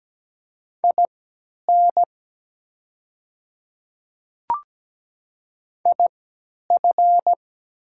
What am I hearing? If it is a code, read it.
Morse code, heard as a keyed tone, about 17 wpm, spaced as for 10 wpm, 700 Hz. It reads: IN IF
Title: 3x Character Spacing - Top 200 Words - 17wpm